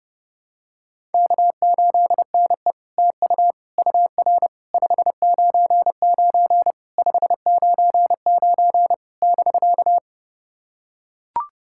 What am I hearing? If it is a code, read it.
Morse code, heard as a keyed tone, about 30 wpm, 700 Hz. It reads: K8NE TU UR 599 599 <BK>
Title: SOTA QSO Practice 30wpm